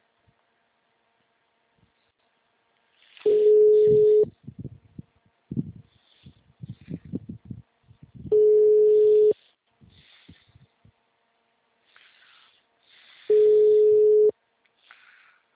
Ringback tone (Europe, 425 Hz).